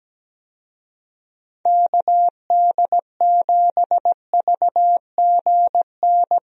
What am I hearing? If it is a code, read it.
Morse code, heard as a keyed tone, about 17 wpm, 700 Hz. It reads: KD7VGN
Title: USA Call Signs - 6 Characters - Repeated 2x - 17wpm